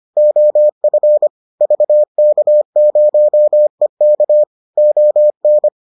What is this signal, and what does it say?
Morse code, heard as a keyed tone, about 25 wpm, 600 Hz. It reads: OF VK0EK ON